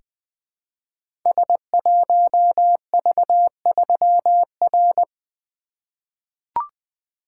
Morse code, heard as a keyed tone, about 21 wpm, 700 Hz. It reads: S1V3R
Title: Code Groups - 5 Letter:Numbers - 20wpm